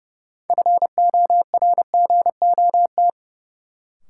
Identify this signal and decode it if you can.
Morse code, heard as a keyed tone, about 30 wpm, 700 Hz. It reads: FORGOT
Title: Sentences from Top 1400 Words - New Word Drill Alt - 30wpm